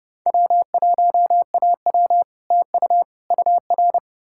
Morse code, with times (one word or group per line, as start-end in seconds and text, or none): 0.3-2.2 W1AW
2.5-3.0 TU
3.3-4.0 UR